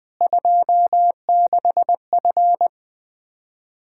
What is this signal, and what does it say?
Morse code, heard as a keyed tone, about 20 wpm, 700 Hz. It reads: 26F
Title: Practice QSO - Exchange 2 - Info - 20wpm